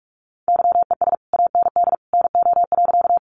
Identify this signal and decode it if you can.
Morse code, heard as a keyed tone, about 44 wpm, 700 Hz. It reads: YES AND NO.
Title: Sentences from Top 500 Words 45wpm